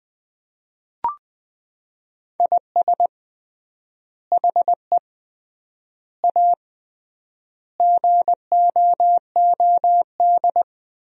Morse code, heard as a keyed tone, about 20 wpm, 700 Hz. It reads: IS HE A GOOD
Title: 3x Word-Spacing - Sentences from Top 200 Words - 20wpm